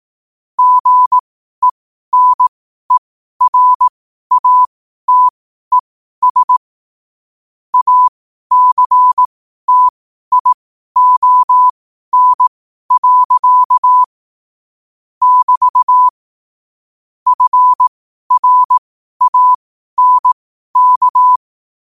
Morse code, heard as a keyed tone, about 18 wpm, spaced as for 12 wpm, 1000 Hz. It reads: GENERATES ACTION. <BT> FRANK
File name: qotdfiles / 12wpm